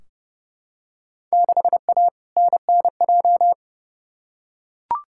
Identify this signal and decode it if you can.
Morse code, heard as a keyed tone, about 30 wpm, 700 Hz. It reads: 6A NNJ
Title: Field Day Exchanges Encore - 30wpm